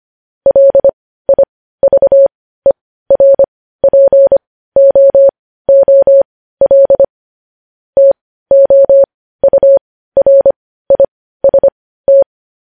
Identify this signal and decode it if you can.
Morse code, heard as a keyed tone, about 25 wpm, spaced as for 15 wpm, 550 Hz. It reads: LIVERPOOL TOURIST
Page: news-headlines-in-morse-code-at-15-wpm